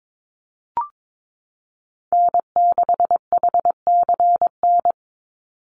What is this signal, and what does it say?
Morse code, heard as a keyed tone, about 22 wpm, 700 Hz. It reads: N6HCN
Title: Call Signs Repeated 2x - 22wpm